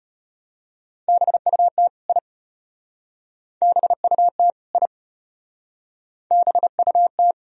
Morse code, heard as a keyed tone, about 31 wpm, 700 Hz. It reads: BUT I BUT I BUT
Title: Speed-Racing - Sets of 2 Words 25wpm